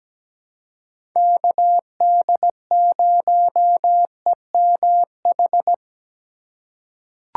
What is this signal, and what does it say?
Morse code, heard as a keyed tone, about 17 wpm, 700 Hz. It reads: KD0EMH
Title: USA Call Signs - 6 Characters - 3x Character Spacing - 17wpm